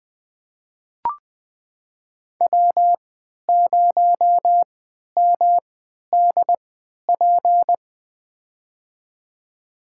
Morse code, heard as a keyed tone, about 20 wpm, spaced as for 11 wpm, 700 Hz. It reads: W0MDP
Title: USA Call Signs - 5 Characters - 3x Character Spacing - 20wpm